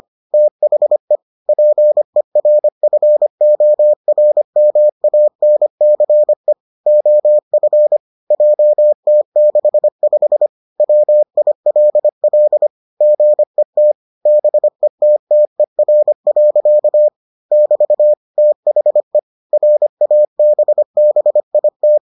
Morse code, heard as a keyed tone, about 25 wpm, 600 Hz. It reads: THE PERFORMANCE OF JT65 WILL GET BETTER. <BT> THE RABBIT